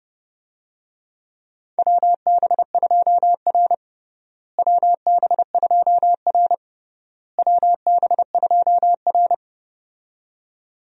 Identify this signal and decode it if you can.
Morse code, heard as a keyed tone, about 30 wpm, 700 Hz. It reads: WB2R WB2R WB2R
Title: USA Call Signs - 4 Characters - Repeated 3x - 30wpm